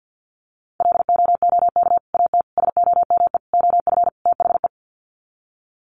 Morse code, heard as a keyed tone, about 50 wpm, 700 Hz. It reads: LOOK AT SOME OF THE
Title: Sets of 5 Words Encore 50wpm